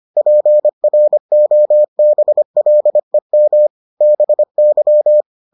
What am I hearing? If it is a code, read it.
Morse code, heard as a keyed tone, about 25 wpm, 600 Hz. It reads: PROBLEM BY